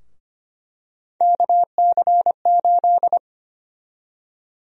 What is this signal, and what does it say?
Morse code, heard as a keyed tone, about 25 wpm, 700 Hz. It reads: KC8
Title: USA Call Sign Prefixes - Repeated 2x - 25wpm